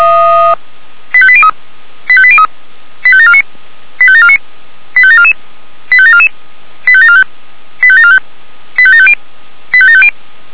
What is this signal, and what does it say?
Five-tone selective call (ZVEI), heard as quick runs of five tones, tones of about 70 ms. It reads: 85803 85803 85749 85749 85744 85744 85745 85745 85750 85750